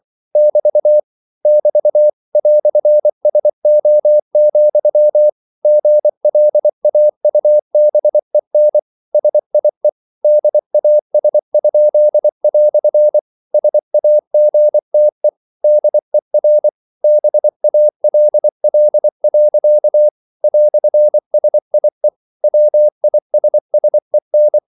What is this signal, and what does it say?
Morse code, heard as a keyed tone, about 24 wpm, 600 Hz. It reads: <BT> <BT> "SO, GLAUBEN SIE DAS?" SAGTE DER BALL. "SIE WISSEN